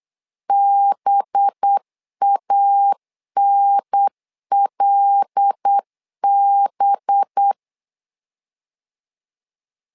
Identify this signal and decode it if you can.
Morse code, heard as a keyed tone, about 8 wpm, 800 Hz. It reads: BANLB
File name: LEZIONI / 5 LBVC / 5 LB 8wpm 2816